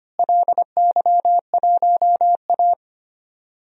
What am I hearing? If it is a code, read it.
Morse code, heard as a keyed tone, about 25 wpm, 700 Hz. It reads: LY1A
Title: DXCC-100 Call Signs - 4 Characters - Repeated 3x - 25wpm